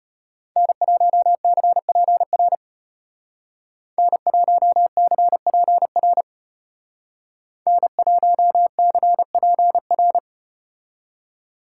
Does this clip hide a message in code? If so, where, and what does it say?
Yes: Morse code from 0.6 to 10.2 s, heard as a keyed tone, about 34 wpm, 700 Hz. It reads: N1CPR N1CPR N1CPR